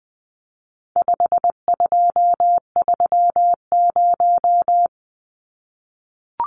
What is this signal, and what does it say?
Morse code, heard as a keyed tone, about 20 wpm, 700 Hz. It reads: 5230
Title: Numbers - 4-Digit 20wpm